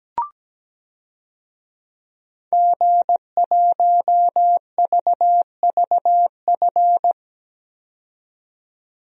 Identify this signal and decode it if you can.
Morse code, heard as a keyed tone, about 17 wpm, 700 Hz. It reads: G1VVF
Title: DXCC-100 Call Signs - 5 Characters - 17wpm